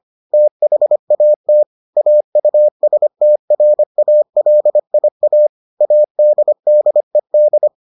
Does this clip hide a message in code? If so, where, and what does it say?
Yes: Morse code from 0.3 to 7.7 s, heard as a keyed tone, about 25 wpm, 600 Hz. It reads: THAT AUSTRALIA ADDED